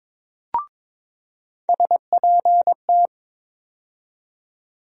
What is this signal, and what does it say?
Morse code, heard as a keyed tone, about 22 wpm, 700 Hz. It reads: SPT
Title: USA Call Sign Postfixes - 3 Letters 22wpm